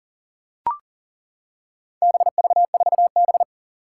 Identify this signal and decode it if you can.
Morse code, heard as a keyed tone, about 41 wpm, 700 Hz. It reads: BV4B
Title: Four Challenging Letter-Number Combinations 40wpm